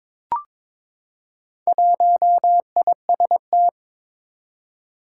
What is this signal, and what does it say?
Morse code, heard as a keyed tone, about 21 wpm, 700 Hz. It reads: 1IST
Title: Four Challenging Letter-Number Combinations 22wpm